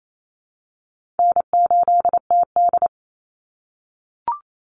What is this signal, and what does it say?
Morse code, heard as a keyed tone, about 28 wpm, 700 Hz. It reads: N8TD
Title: Call Signs - 28wpm